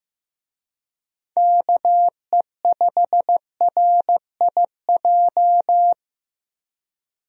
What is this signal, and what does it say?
Morse code, heard as a keyed tone, about 15 wpm, 700 Hz. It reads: KE5RIJ
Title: USA Call Signs - 6 Characters - 15wpm